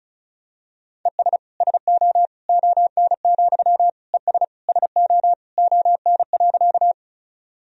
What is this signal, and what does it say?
Morse code, heard as a keyed tone, about 34 wpm, 700 Hz. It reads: ES SO ON, ES SO ON.
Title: Sentences from Top 100 Words and Abbreviations 35wpm